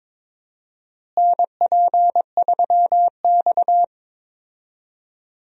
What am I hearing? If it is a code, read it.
Morse code, heard as a keyed tone, about 22 wpm, 700 Hz. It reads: NP3X
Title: USA Call Signs - 4 Characters - Repeated 2x - 22wpm